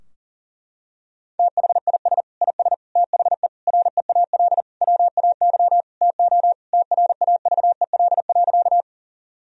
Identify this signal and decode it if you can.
Morse code, heard as a keyed tone, about 40 wpm, 700 Hz. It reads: THIS IS THE REAL WAY TO TRAVEL.
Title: Sentences from Top 1000 Words - Review Part 2 - 40wpm